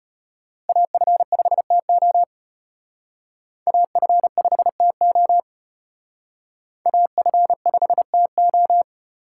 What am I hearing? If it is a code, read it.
Morse code, heard as a keyed tone, about 31 wpm, 700 Hz. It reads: AF5TO AF5TO AF5TO